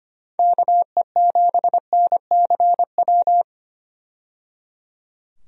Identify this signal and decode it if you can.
Morse code, heard as a keyed tone, about 25 wpm, 700 Hz. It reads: KE7NCW
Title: USA Call Signs - 6 Characters - 25wpm